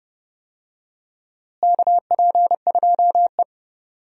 Morse code, heard as a keyed tone, about 30 wpm, 700 Hz. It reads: KP2E
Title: DXCC-100 Call Signs - 4 Characters - Repeated 3x - 30wpm